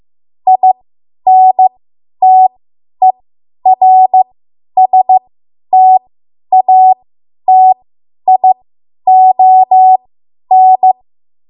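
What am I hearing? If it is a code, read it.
Morse code, heard as a keyed tone, about 15 wpm, spaced as for 10 wpm, 750 Hz. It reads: INTERSTATION
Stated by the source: Morse code practice run at 10 WPM from a code practice archive.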